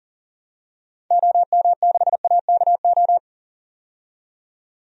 Morse code, heard as a keyed tone, about 40 wpm, 700 Hz. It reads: OM6AKO
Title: DXCC-100 Call Signs - 6 Characters - 40wpm